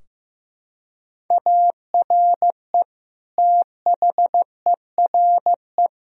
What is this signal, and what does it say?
Morse code, heard as a keyed tone, about 15 wpm, 700 Hz. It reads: ARE THERE
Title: Sentences from Top 1000 Words - Review Part 4 - 15wpm